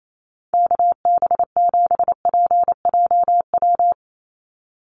Morse code, heard as a keyed tone, about 28 wpm, 700 Hz. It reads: KB7PJW
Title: USA Call Signs 28wpm